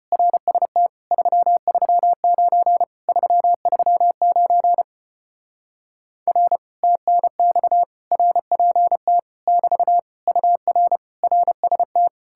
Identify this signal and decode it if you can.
Morse code, heard as a keyed tone, about 33 wpm, 700 Hz. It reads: RST 339 339 R TNX RPT <BT> UR RST